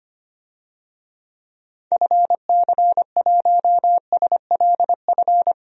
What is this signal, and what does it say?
Morse code, heard as a keyed tone, about 25 wpm, 700 Hz. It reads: FC1SLF
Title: DXCC-100 Call Signs - 6 Characters - 25wpm